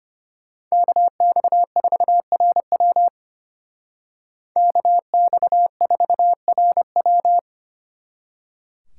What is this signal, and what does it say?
Morse code, heard as a keyed tone, about 33 wpm, 700 Hz. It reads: KX4RW KX4RW